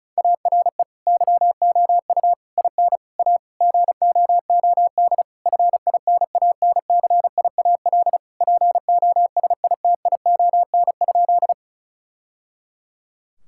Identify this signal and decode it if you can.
Morse code, heard as a keyed tone, about 36 wpm, 700 Hz. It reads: ARE YOU IN A GOOD FINANCIAL POSITION?
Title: Sentences from Top 1000 Words - Review Part 5 - 35wpm